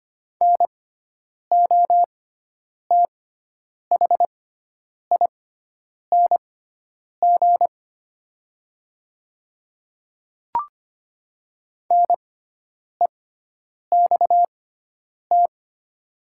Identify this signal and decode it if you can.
Morse code, heard as a keyed tone, about 25 wpm, spaced as for 9 wpm, 700 Hz. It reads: NOTHING NEXT
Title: 6x Character Spacing - Top 500 Words - 25wpm